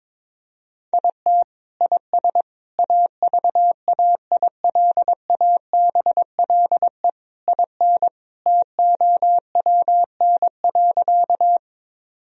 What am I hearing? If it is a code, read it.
Morse code, heard as a keyed tone, about 22 wpm, 700 Hz. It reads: IT IS AVAILABLE IN TOWN.
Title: Sentences from Top 1000 Words - Review Part 3 - 22wpm